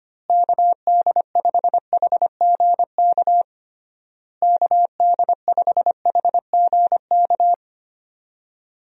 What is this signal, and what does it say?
Morse code, heard as a keyed tone, about 25 wpm, 700 Hz. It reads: KD5HGK KD5HGK
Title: USA Call Signs - 6 Characters - Repeated 3x - 25wpm